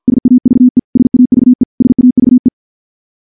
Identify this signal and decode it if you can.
Morse code, heard as a keyed tone, about 50 wpm, 250 Hz. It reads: SAVE SAVE SAVE
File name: morse / morse-training-50WPM